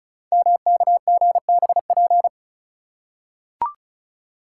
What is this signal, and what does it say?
Morse code, heard as a keyed tone, about 36 wpm, 700 Hz. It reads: MKGBP
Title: Code Groups - 5 Letters - 35wpm